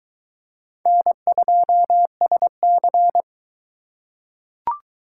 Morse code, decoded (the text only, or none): N2SC